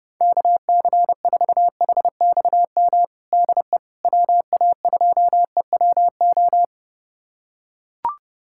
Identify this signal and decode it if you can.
Morse code, heard as a keyed tone, about 30 wpm, 700 Hz. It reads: KC4HXM DE WA2EWO
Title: USA Callsign DE Callsign 30wpm